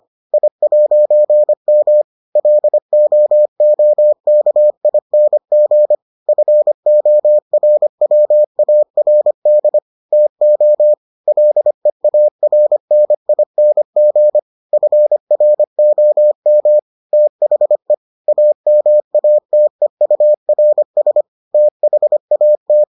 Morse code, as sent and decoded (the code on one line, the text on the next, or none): .. .----. -- / .-.. --- --- -.- .. -. --. / ..-. --- .-. .-- .- .-. -.. / - --- / .-.. .- .-. -. .. -. --. / ..-. .-. --- -- / - .... / .- -- .- - . ..- .-. ... / - .... .- -
I'M LOOKING FORWARD TO LEARNING FROM THE AMATEURS THAT